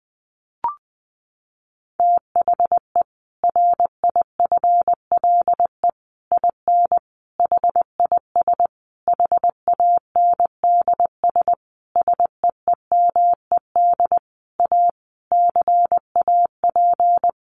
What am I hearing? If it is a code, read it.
Morse code, heard as a keyed tone, about 20 wpm, 700 Hz. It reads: THE RIFLE IN HIS HANDS SEEMED A CAP
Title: A Sound Of Thunder 20wpm